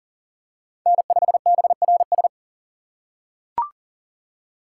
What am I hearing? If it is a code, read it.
Morse code, heard as a keyed tone, about 40 wpm, 700 Hz. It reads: N5BRS
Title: USA Call Signs - 3x Character Spacing - 40wpm